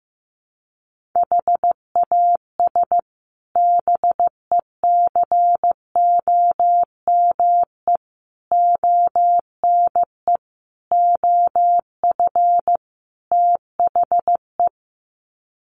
Morse code, heard as a keyed tone, about 15 wpm, 700 Hz. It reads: HAS BECOME ONE OF THE
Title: Sets of 5 Words Encore 15wpm